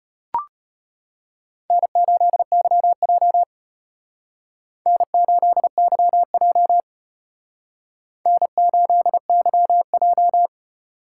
Morse code, heard as a keyed tone, about 33 wpm, 700 Hz. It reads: N8YJ N8YJ N8YJ